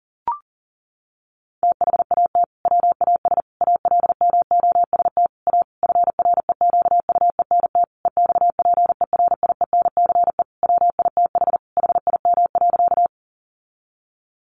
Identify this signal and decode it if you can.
Morse code, heard as a keyed tone, about 40 wpm, 700 Hz. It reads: THAT WAS ALMOST A FREQUENT EXPERIENCE WITH HIM.
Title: Sentences from Top 1400 Words 40wpm